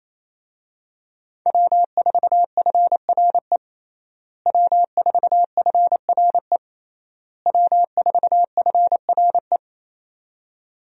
Morse code, heard as a keyed tone, about 28 wpm, 700 Hz. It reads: W4FRE W4FRE W4FRE
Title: USA Call Signs - 5 Characters - Repeated 3x - 28wpm